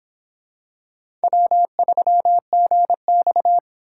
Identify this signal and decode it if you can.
Morse code, heard as a keyed tone, about 26 wpm, 700 Hz. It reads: W3GX